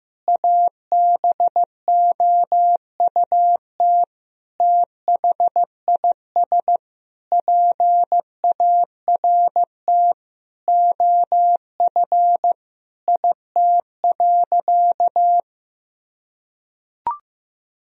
Morse code, heard as a keyed tone, about 15 wpm, 700 Hz. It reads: ABOUT THIS PART OF IT.
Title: Sentences from Top 1000 Words - Review Part 1 - 15wpm